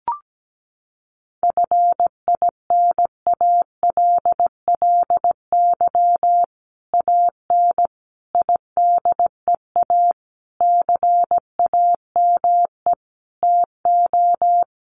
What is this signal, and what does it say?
Morse code, heard as a keyed tone, about 17 wpm, 700 Hz. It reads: FINALLY AN IDEA CAME TO
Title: Sentences from Top 1000 Words 17wpm